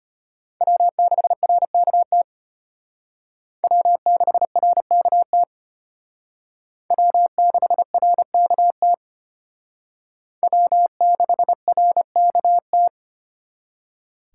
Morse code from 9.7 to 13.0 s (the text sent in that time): W6RKT